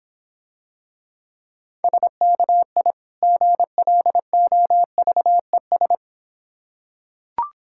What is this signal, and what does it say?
Morse code, heard as a keyed tone, about 25 wpm, 700 Hz. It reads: SKI GLOVES